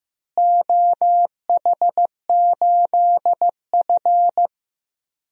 Morse code, heard as a keyed tone, about 15 wpm, 700 Hz. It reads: OH8F
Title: DXCC-100 Call Signs - 4 Characters - Repeated 3x - 15wpm